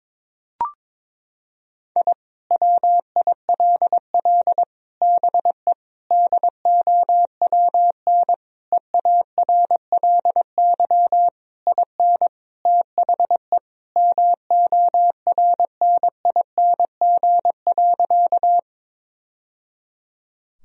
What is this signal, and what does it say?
Morse code, heard as a keyed tone, about 22 wpm, 700 Hz. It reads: I WILL BE DOWN EARLY IN THE MORNING.
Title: Sentences from Top 1000 Words - Review Part 2 - 22wpm